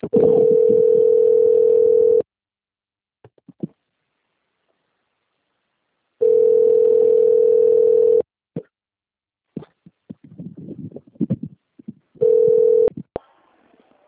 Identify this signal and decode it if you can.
Ringback tone (North America), 440 + 480 Hz.
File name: out- - - - - .36573